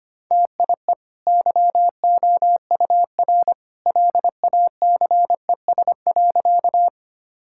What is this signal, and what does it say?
Morse code, heard as a keyed tone, about 25 wpm, 700 Hz. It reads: TIE YOUR LACES.
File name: Easy Sentences with Hints - 25wpm